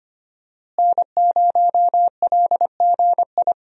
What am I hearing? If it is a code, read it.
Morse code, heard as a keyed tone, about 25 wpm, 700 Hz. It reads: N0LGI
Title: USA Call Signs - 5 Characters - Repeated 2x - 25wpm